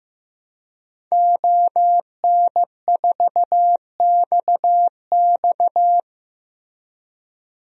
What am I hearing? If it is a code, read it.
Morse code, heard as a keyed tone, about 16 wpm, 700 Hz. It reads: ON4XX
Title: DXCC-100 Call Signs - 5 Characters - Repeated 2x - 15wpm